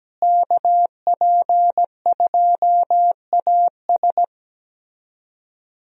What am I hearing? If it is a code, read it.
Morse code, heard as a keyed tone, about 17 wpm, 700 Hz. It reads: KP2AS
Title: USA Call Signs - 5 Characters - Repeated 3x - 17wpm